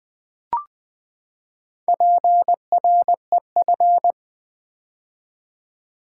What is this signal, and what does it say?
Morse code, heard as a keyed tone, about 20 wpm, 700 Hz. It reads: PREF
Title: Build A Word - 4 Letters At A Time - 10-Letter Words - 20wpm